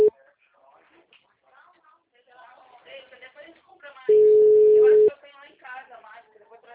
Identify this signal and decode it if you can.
Ringback tone (Europe, 425 Hz), with speech.